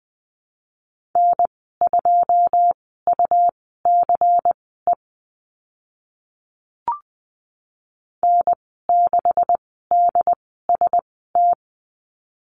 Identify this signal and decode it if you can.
Morse code, heard as a keyed tone, about 20 wpm, spaced as for 15 wpm, 700 Hz. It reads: N2UCE N6DST
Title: 140 Lesson 23g - Letter f - Call Signs 20wpm